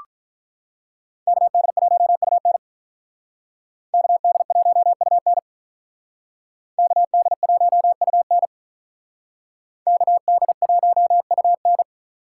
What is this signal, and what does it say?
Morse code, heard as a keyed tone, about 46 wpm, 700 Hz. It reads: KD1UN KD1UN KD1UN KD1UN